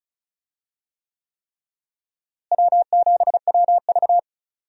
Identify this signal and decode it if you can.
Morse code, heard as a keyed tone, about 34 wpm, 700 Hz. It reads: W7WV
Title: USA Call Signs - 4 Characters - Repeated 3x - 35wpm